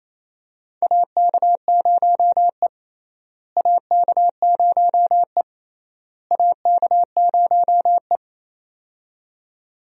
Morse code, heard as a keyed tone, about 28 wpm, 700 Hz. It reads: AK0E AK0E AK0E
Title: USA Call Signs - 4 Characters - Repeated 3x - 28wpm